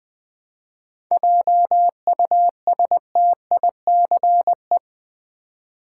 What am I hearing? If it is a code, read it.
Morse code, heard as a keyed tone, about 20 wpm, 700 Hz. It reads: JUSTICE
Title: 7 Letter Words - First 3 Letters Given - 20wpm